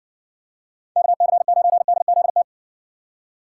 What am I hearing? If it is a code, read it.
Morse code, heard as a keyed tone, about 55 wpm, 700 Hz. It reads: KC9DZT